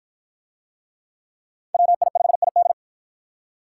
Morse code, heard as a keyed tone, about 53 wpm, 700 Hz. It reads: WI5ID